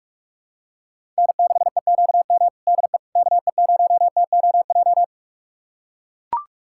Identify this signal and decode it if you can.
Morse code, heard as a keyed tone, about 44 wpm, 700 Hz. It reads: N6EQM DE KE0TOJ
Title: USA Callsign DE Callsign 45wpm